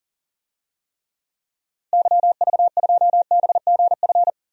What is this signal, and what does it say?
Morse code, heard as a keyed tone, about 40 wpm, 700 Hz. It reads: YV2BGF
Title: DXCC-100 Call Signs - 6 Characters - Repeated 2x - 40wpm